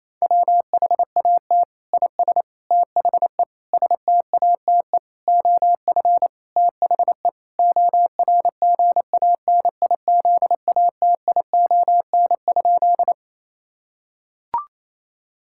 Morse code, heard as a keyed tone, about 28 wpm, 700 Hz. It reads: WHAT IS THE STATE OF THE ORGANIZATION?
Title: Sentences from Top 1000 Words - Review Part 5 - 28wpm